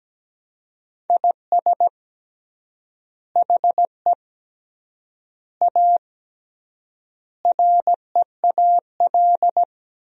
Morse code, heard as a keyed tone, about 17 wpm, 700 Hz. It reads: IS HE A REAL
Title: 3x Word-Spacing - Sentences from Top 300 Words - 17wpm